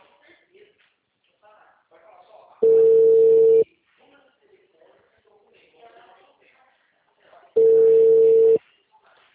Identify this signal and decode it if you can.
Ringback tone (China, 450 Hz), with speech.